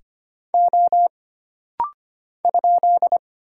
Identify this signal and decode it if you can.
Morse code, heard as a keyed tone, about 25 wpm, 700 Hz. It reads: O ?
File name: Single Character (Letter:Number:Punctuation:Prosign) - Rapid-Fire - 25wpm